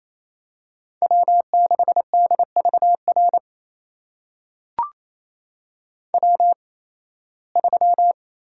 Morse code, heard as a keyed tone, about 28 wpm, spaced as for 18 wpm, 700 Hz. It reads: W6DVR W3
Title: USA Call Signs - 5 Characters - 8x Character Spacing - 28wpm